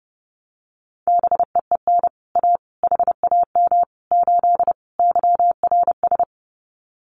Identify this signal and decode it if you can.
Morse code, heard as a keyed tone, about 30 wpm, 700 Hz. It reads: BEEN A HAM 8 YRS
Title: Common QSO Phrases Encore 30wpm